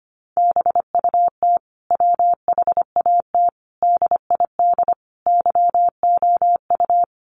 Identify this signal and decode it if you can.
Morse code, heard as a keyed tone, about 25 wpm, 700 Hz. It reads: BUT WHAT DID YOU
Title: Sentences from Top 1300 Words 25wpm